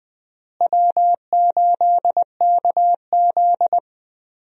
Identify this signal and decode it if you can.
Morse code, heard as a keyed tone, about 20 wpm, 700 Hz. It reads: W8KZ
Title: USA Call Signs - 4 Characters - Repeated 2x - 20wpm